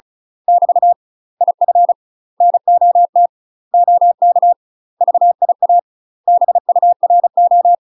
Morse code, heard as a keyed tone, about 36 wpm, 700 Hz. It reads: <BT> IF NOT OK VIA BURO